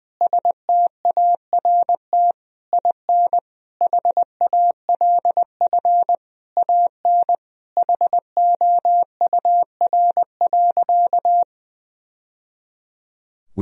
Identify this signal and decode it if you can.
Morse code, heard as a keyed tone, about 20 wpm, 700 Hz. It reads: START IN HALF AN HOUR.
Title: Sentences from Top 1000 Words - Review Part 1 - 20wpm